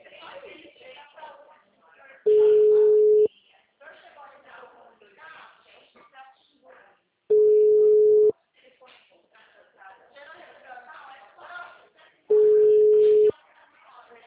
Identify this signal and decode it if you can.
Ringback tone (Europe, 425 Hz), with speech.
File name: out- -2010- - - .85328